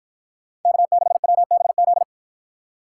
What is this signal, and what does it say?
Morse code, heard as a keyed tone, about 53 wpm, 700 Hz. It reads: K6WBZ